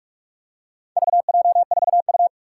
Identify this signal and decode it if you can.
Morse code, heard as a keyed tone, about 44 wpm, 700 Hz. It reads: VJ4U
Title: Four Challenging Letter-Number Combinations 45wpm